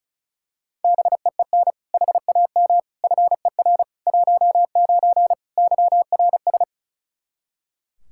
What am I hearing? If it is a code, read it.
Morse code, heard as a keyed tone, about 36 wpm, 700 Hz. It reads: BEEN HAM FER 19 YRS
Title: Practice QSO - Exchange 3 - More Info - 35wpm